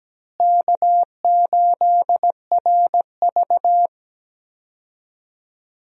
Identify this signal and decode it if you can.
Morse code, heard as a keyed tone, about 17 wpm, 700 Hz. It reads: K8RV
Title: USA Call Signs - 4 Characters - Repeated 2x - 17wpm